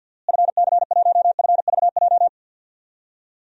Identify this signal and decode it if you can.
Morse code, heard as a keyed tone, about 52 wpm, 700 Hz. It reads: FC1FVJ
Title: DXCC-100 Call Signs - 6 Characters - Repeated 2x - 50wpm